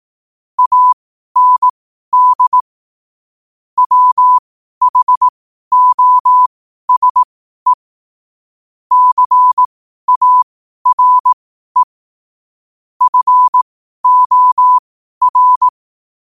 Morse code, heard as a keyed tone, about 18 wpm, spaced as for 12 wpm, 1000 Hz. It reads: AND WHOSE CARE FOR
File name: qotdfiles / 12wpm